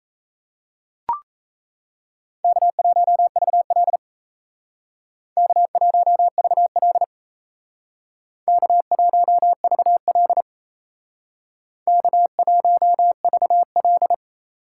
Morse code, heard as a keyed tone, about 35 wpm, 700 Hz. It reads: K1VL K1VL K1VL K1VL
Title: Speed Racing - USA 1x2 Call Signs 28wpm